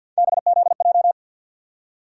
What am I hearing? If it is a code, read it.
Morse code, heard as a keyed tone, about 53 wpm, 700 Hz. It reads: BZJ
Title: USA Call Sign Postfixes - 3 Letters 50wpm